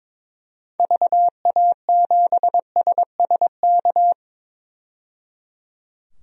Morse code, heard as a keyed tone, about 21 wpm, 700 Hz. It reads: VA7SSK
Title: DXCC-100 Call Signs - 6 Characters - 22wpm